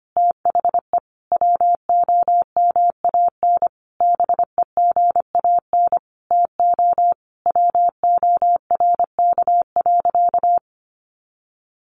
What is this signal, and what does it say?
Morse code, heard as a keyed tone, about 25 wpm, 700 Hz. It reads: THE WOMAN BEGAN TO WORK.
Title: Sentences from Top 1100 Words 25wpm